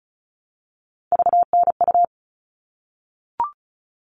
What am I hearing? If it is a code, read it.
Morse code, heard as a keyed tone, about 34 wpm, 700 Hz. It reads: VNU
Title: All Cut Numbers - 3-Digit 35wpm